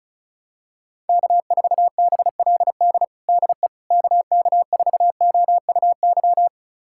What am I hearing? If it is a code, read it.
Morse code, heard as a keyed tone, about 34 wpm, 700 Hz. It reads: K4BLD DE KK4OUY
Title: USA Callsign DE Callsign 35wpm